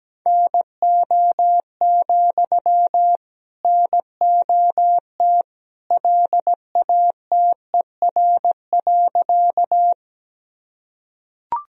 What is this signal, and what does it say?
Morse code, heard as a keyed tone, about 17 wpm, 700 Hz. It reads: NO, NOT LATER.
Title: Sentences from Top 1000 Words - Review Part 4 - 17wpm